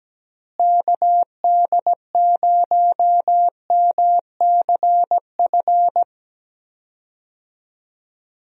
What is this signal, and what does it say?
Morse code, heard as a keyed tone, about 17 wpm, 700 Hz. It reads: KD0MCF
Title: USA Call Signs - 6 Characters - Repeated 2x - 17wpm